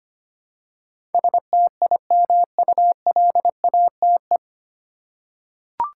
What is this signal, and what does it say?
Morse code, heard as a keyed tone, about 25 wpm, 700 Hz. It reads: STIMULATE
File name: practice / 25wpm / 9 Letter Words - First 3 Letters Given - 25wpm